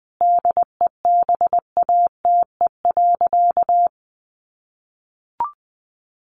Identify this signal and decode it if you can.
Morse code, heard as a keyed tone, about 20 wpm, 700 Hz. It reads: DEBATE.
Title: Sentences from Top 1100 Words 20wpm